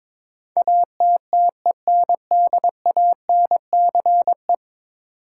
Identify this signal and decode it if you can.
Morse code, heard as a keyed tone, about 22 wpm, 700 Hz. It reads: ATTENDANCE
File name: practice / 22wpm / 10 Letter Words - First 3 Letters Given - 22wpm